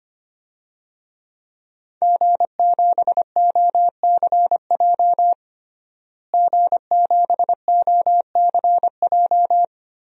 Morse code, heard as a keyed tone, about 25 wpm, 700 Hz. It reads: G7OCJ G7OCJ
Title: DXCC-100 Call Signs - 5 Characters - Repeated 3x - 25wpm